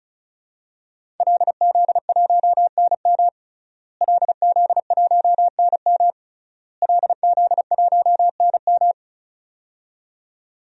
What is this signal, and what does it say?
Morse code, heard as a keyed tone, about 34 wpm, 700 Hz. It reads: LZ1NM LZ1NM LZ1NM
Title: DXCC-100 Call Signs - 5 Characters - Repeated 3x - 35wpm